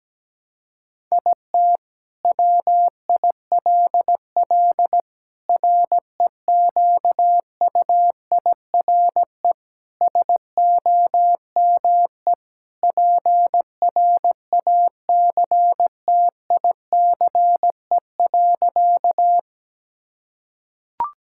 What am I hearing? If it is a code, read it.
Morse code, heard as a keyed tone, about 17 wpm, 700 Hz. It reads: IT WILL REQUIRE SOME PRACTICE.
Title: Sentences from Top 1000 Words - Review Part 6 - 17wpm